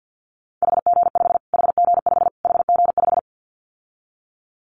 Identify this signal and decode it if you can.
Morse code, heard as a keyed tone, about 50 wpm, 700 Hz. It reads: HG5 HG5 HG5
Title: Callsign Prefixes for Top 25 DXCC - Repeated 3x - 50wpm